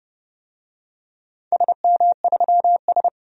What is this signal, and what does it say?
Morse code, heard as a keyed tone, about 31 wpm, 700 Hz. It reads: SM3S
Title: DXCC-100 Call Signs - 4 Characters - 30wpm